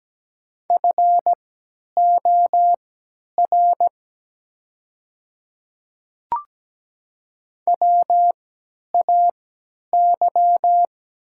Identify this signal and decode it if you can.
Morse code, heard as a keyed tone, about 17 wpm, spaced as for 10 wpm, 700 Hz. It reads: FOR WAY
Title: 3x Character Spacing - Top 100 Words - 17wpm